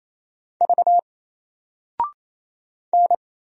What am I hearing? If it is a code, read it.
Morse code, heard as a keyed tone, about 28 wpm, 700 Hz. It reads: V N
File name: Single-Letter-Number-28wpm